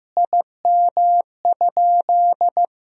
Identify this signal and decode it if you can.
Morse code, heard as a keyed tone, about 16 wpm, 700 Hz. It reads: IM?
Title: Sentences from Top 400 Words and Abbreviations 15wpm